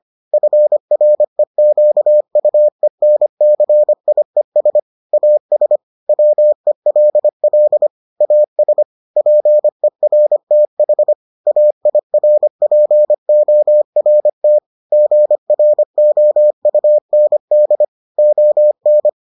Morse code, heard as a keyed tone, about 25 wpm, 600 Hz. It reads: FREQUENCIES AS WELL AS PERTH AIRPORT GROUND ON